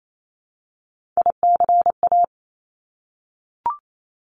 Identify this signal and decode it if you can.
Morse code, heard as a keyed tone, about 28 wpm, 700 Hz. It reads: ICA
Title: Top Three Letter Combinations in Words 28wpm